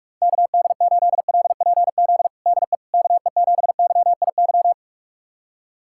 Morse code, heard as a keyed tone, about 41 wpm, 700 Hz. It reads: KD8LPZ DE KE7YIY
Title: USA Callsign DE Callsign - 6 Characters - 45wpm